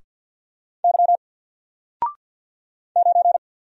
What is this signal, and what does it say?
Morse code, heard as a keyed tone, about 48 wpm, 700 Hz. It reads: Y 9
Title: Single Letter-Number 50wpm